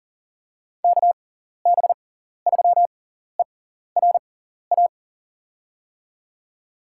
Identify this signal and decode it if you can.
Morse code, heard as a keyed tone, about 40 wpm, spaced as for 14 wpm, 690 Hz. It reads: KB3ERA
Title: USA Call Signs - 6x Character Spacing - 40wpm